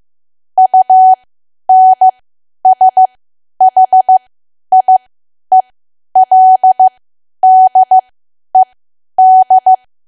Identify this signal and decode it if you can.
Morse code, heard as a keyed tone, about 15 wpm, spaced as for 10 wpm, 750 Hz. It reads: UNSHIELDED
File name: Archive / 10 WPM / 10WPM